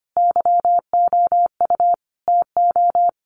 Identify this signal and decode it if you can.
Morse code, heard as a keyed tone, about 25 wpm, 700 Hz. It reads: YOU TO
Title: Sentences from Top 700 Words 25wpm